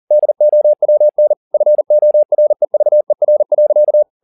Morse code, heard as a keyed tone, about 40 wpm, 600 Hz. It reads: DOWN FOREVER.